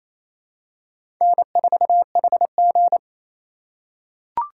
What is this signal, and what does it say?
Morse code, heard as a keyed tone, about 28 wpm, 700 Hz. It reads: N4HG
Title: USA Call Signs - 4 Characters - 3x Character Spacing - 28wpm